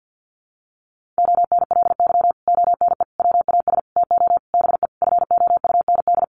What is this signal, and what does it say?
Morse code, heard as a keyed tone, about 50 wpm, 700 Hz. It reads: ONLY ONE WAS TO BE FOUND
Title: Sentences from Top 1400 Words 50wpm